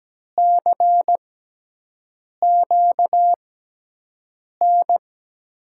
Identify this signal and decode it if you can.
Morse code, heard as a keyed tone, about 17 wpm, spaced as for 6 wpm, 700 Hz. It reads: CQN
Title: USA Call Signs - 5 Characters - 6x Character Spacing - 17wpm